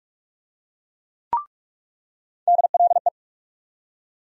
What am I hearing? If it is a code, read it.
Morse code, heard as a keyed tone, about 47 wpm, 700 Hz. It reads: DLE
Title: USA Call Sign Postfixes - 3 Letters 45wpm